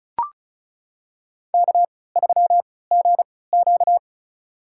Morse code, heard as a keyed tone, about 34 wpm, spaced as for 20 wpm, 700 Hz. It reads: K3GQ
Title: USA Call Signs - 4 Characters - 3x Character Spacing - 35wpm